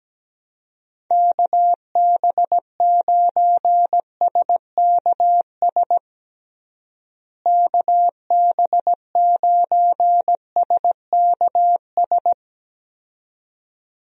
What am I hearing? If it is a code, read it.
Morse code, heard as a keyed tone, about 17 wpm, 700 Hz. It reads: KB9SKS KB9SKS
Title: USA Call Signs - 6 Characters - Repeated 2x - 17wpm